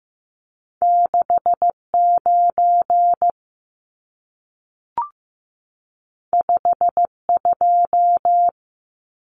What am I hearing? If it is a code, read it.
Morse code, heard as a keyed tone, about 15 wpm, 700 Hz. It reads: 69 52
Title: Numbers - 2-Digit 15wpm